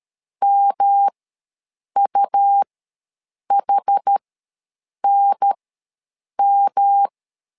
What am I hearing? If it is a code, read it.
Morse code, heard as a keyed tone, about 13 wpm, spaced as for 7 wpm, 800 Hz. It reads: MUHNM
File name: LEZIONI / 3 MWRO / 3 M ws 12wpm 1745